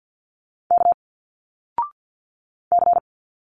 Morse code, heard as a keyed tone, about 50 wpm, 700 Hz. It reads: K C
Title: Single Letters 50wpm